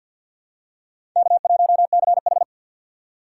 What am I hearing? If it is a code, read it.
Morse code, heard as a keyed tone, about 53 wpm, 700 Hz. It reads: K1CH
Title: USA Call Signs - 4 Characters - 3x Character Spacing - 50wpm